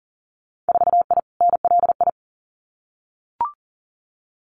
Morse code, heard as a keyed tone, about 37 wpm, 700 Hz. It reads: 4I NLI
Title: Winter Field Day Exchanges Encore 40wpm